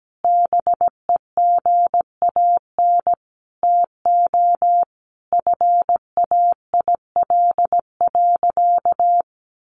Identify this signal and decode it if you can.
Morse code, heard as a keyed tone, about 17 wpm, 700 Hz. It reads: BEGAN TO FAIL.
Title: Sentences from Top 1100 Words 17wpm